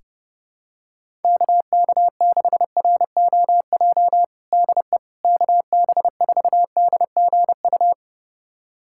Morse code, heard as a keyed tone, about 30 wpm, 700 Hz. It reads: KK6ROJ DE KB4DGU
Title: USA Callsign DE Callsign - 6 Characters - 30wpm